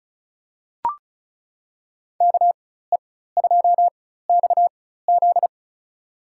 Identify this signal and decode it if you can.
Morse code, heard as a keyed tone, about 36 wpm, spaced as for 16 wpm, 700 Hz. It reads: KE2XZ
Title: USA Call Signs - 5 Characters - 4x Character Spacing - 35wpm